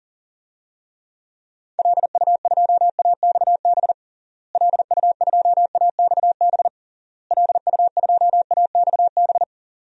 Morse code, heard as a keyed tone, about 40 wpm, 700 Hz. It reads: LU2AXB LU2AXB LU2AXB
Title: DXCC-100 Call Signs - 6 Characters - Repeated 3x - 40wpm